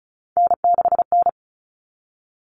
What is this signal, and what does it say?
Morse code, heard as a keyed tone, about 34 wpm, 700 Hz. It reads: N6N
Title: All Cut Numbers - 3-Digit 35wpm